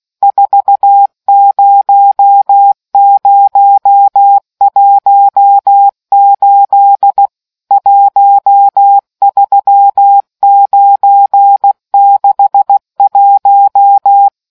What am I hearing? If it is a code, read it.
Morse code, heard as a keyed tone, about 16 wpm, 800 Hz. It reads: 40018 13961